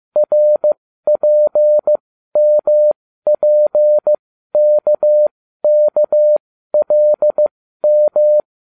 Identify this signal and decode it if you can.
Morse code, heard as a keyed tone, about 15 wpm, spaced as for 12 wpm, 600 Hz. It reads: RPMPKKLM